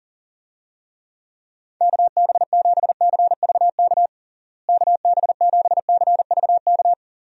Morse code, heard as a keyed tone, about 40 wpm, 700 Hz. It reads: KB7CVK KB7CVK
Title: USA Call Signs - 6 Characters - Repeated 2x - 40wpm